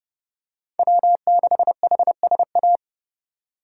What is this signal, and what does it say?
Morse code, heard as a keyed tone, about 30 wpm, 700 Hz. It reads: W6HSA
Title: USA Call Signs - 5 Characters - 4x Character Spacing - 30wpm